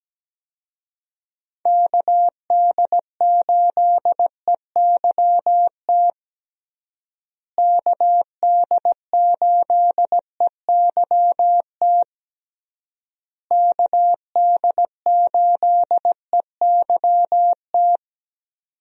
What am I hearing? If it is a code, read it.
Morse code, heard as a keyed tone, about 17 wpm, 700 Hz. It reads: KD8EYT KD8EYT KD8EYT
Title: USA Call Signs - 6 Characters - Repeated 3x - 17wpm